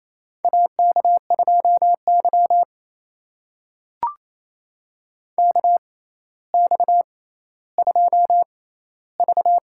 Morse code, heard as a keyed tone, about 27 wpm, spaced as for 14 wpm, 700 Hz. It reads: AK2Y KX2V
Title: USA Call Signs - 4 Characters - 6x Character Spacing - 28wpm